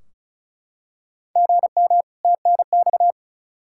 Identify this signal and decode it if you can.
Morse code, heard as a keyed tone, about 36 wpm, 700 Hz. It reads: GM TNX
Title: Practice QSO - Exchange 1 - Beginning - 35wpm